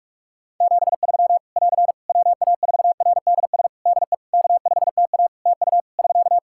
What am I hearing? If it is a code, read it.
Morse code, heard as a keyed tone, about 44 wpm, 700 Hz. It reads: 73 <AR> WA4RDS DE K5TA TU <SK>
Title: Practice QSO - Exchange 4 - Ending - 45wpm